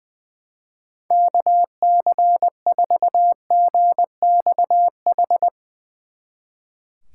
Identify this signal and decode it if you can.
Morse code, heard as a keyed tone, about 22 wpm, 700 Hz. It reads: KC4GXH
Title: Speed Racing - USA 2x3 Call Signs 20wpm